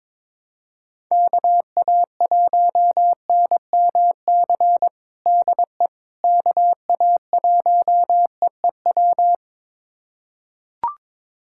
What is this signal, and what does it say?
Morse code, heard as a keyed tone, about 22 wpm, 700 Hz. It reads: KA1NMC DE KA1EEW
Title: USA Callsign DE Callsign 22wpm